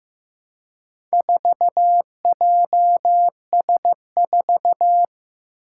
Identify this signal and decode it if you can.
Morse code, heard as a keyed tone, about 15 wpm, 700 Hz. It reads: 4JS4
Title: Four Challenging Letter-Number Combinations 15wpm